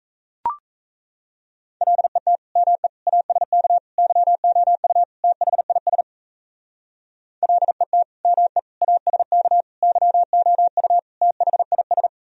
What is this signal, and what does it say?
Morse code, heard as a keyed tone, about 35 wpm, 700 Hz. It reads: LET ME ASK YOU THIS LET ME ASK YOU THIS
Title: Speed-Racing - Sets of 5 Words 28wpm